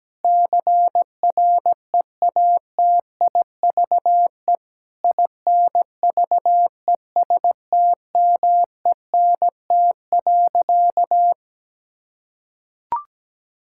Morse code, heard as a keyed tone, about 17 wpm, 700 Hz. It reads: CREATIVE INVESTMENT.
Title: Sentences from Top 1000 Words - Review Part 5 - 17wpm